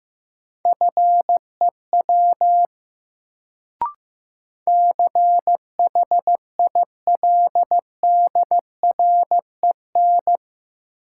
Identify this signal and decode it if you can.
Morse code, heard as a keyed tone, about 15 wpm, 700 Hz. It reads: FEW CHILDREN
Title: Sentences from Top 300 Words - New Word Drill - 15wpm